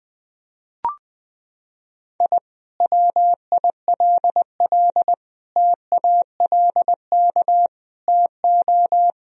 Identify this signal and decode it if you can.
Morse code, heard as a keyed tone, about 20 wpm, 700 Hz. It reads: I WILL TALK TO
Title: Sentences from Top 1000 Words - Review Part 2 - 20wpm